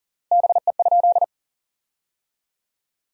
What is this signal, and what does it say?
Morse code, heard as a keyed tone, about 40 wpm, 700 Hz. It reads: BE?
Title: 3x Word-Spacing - Sentences from Top 100 Words - 40wpm